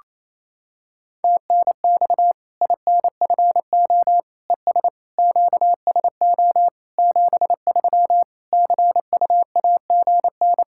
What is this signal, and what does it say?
Morse code, heard as a keyed tone, about 28 wpm, 700 Hz. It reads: TNX INFO ES QSO 73 CUAGN
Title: Practice QSO - Exchange 4 - Ending - 28wpm